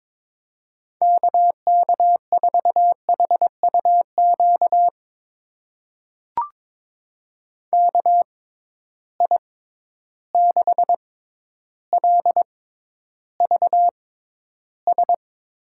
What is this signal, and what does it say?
Morse code, heard as a keyed tone, about 21 wpm, spaced as for 11 wpm, 700 Hz. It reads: KK4HUQ KI6LVS
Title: USA Call Signs - 6 Characters - 6x Character Spacing - 22wpm